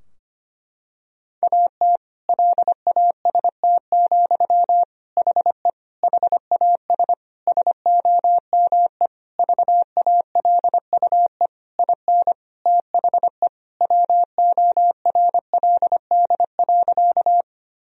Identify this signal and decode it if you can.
Morse code, heard as a keyed tone, about 25 wpm, 700 Hz. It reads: AT LAST, HE HAS SOME VALUE IN THE WORLD.
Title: Sentences from Top 1000 Words - Review Part 3 - 25wpm